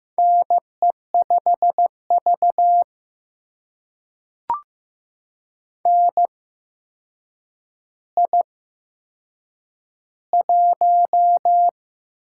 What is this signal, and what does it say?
Morse code, heard as a keyed tone, about 15 wpm, spaced as for 7 wpm, 700 Hz. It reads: NE5V NI1
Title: USA Call Signs - 4 Characters - 8x Character Spacing - 15wpm